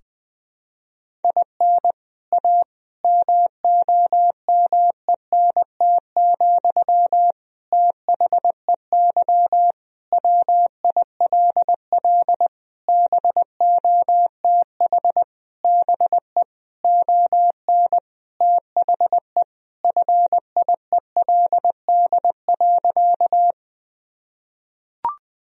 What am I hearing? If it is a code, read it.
Morse code, heard as a keyed tone, about 20 wpm, 700 Hz. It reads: IN A MOMENT, THEY WILL BOTH BE ON THE FIELD.
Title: Sentences from Top 1000 Words - Review Part 2 - 20wpm